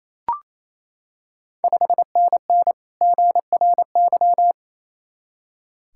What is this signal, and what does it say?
Morse code, heard as a keyed tone, about 28 wpm, 690 Hz. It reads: 5NN GRY
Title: Ontario QSO Party Exchange 28wpm